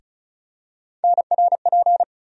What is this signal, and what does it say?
Morse code, heard as a keyed tone, about 36 wpm, 700 Hz. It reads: NRP
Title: USA Call Sign Postfixes - 3 Letters 35wpm